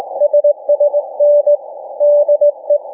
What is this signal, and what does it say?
Morse code, heard as a keyed tone, about 19 wpm, 590 Hz. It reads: SSN DE
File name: cw / ssn 40M